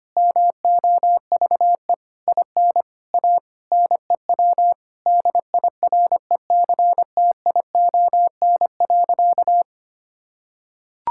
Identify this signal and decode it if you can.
Morse code, heard as a keyed tone, about 25 wpm, 700 Hz. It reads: MOVE IN A NEW DIRECTION.
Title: Sentences from Top 1000 Words - Review Part 5 - 25wpm